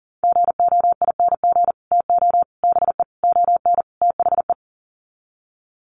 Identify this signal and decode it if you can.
Morse code, heard as a keyed tone, about 40 wpm, 700 Hz. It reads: GOING TO BE ON THE
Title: Sets of 5 Words Encore2 40wpm